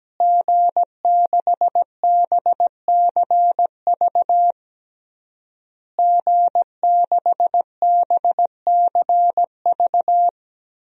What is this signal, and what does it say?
Morse code, heard as a keyed tone, about 17 wpm, 700 Hz. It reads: G6BCV G6BCV
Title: DXCC-100 Call Signs - 5 Characters - Repeated 3x - 17wpm